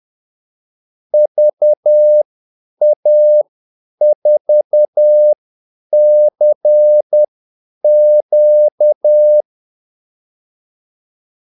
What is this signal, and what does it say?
Morse code, heard as a keyed tone, about 10 wpm, spaced as for 8 wpm, 600 Hz. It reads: VA4CQ
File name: Callsigns-10-8wpm